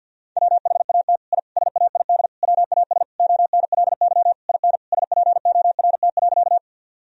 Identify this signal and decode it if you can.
Morse code, heard as a keyed tone, about 50 wpm, 700 Hz. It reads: WHAT I SAID WAS ONLY IN SPORT.
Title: Sentences from Top 1000 Words - Review Part 4 - 50wpm